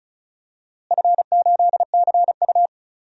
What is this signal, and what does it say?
Morse code, heard as a keyed tone, about 36 wpm, 700 Hz. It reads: F8CU
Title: DXCC-100 Call Signs - 4 Characters - Repeated 2x - 35wpm